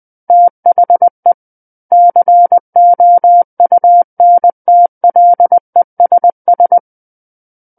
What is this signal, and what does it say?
Morse code, heard as a keyed tone, about 20 wpm, 700 Hz. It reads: THE COUNTLESS
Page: SKCC Learning Center - Morse Code Practice Files